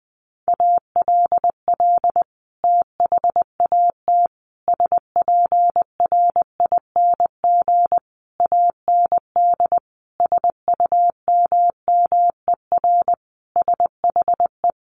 Morse code, heard as a keyed tone, about 20 wpm, 700 Hz. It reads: ALL THAT SPRING AND SUMMER SHE
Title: Sentences from Top 1200 Words 20wpm